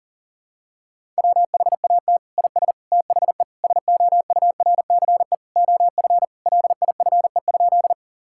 Morse code, heard as a keyed tone, about 37 wpm, 700 Hz. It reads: WHAT IS THE SOURCE OF LIFE?
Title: Sentences from Top 1000 Words - Review Part 5 - 40wpm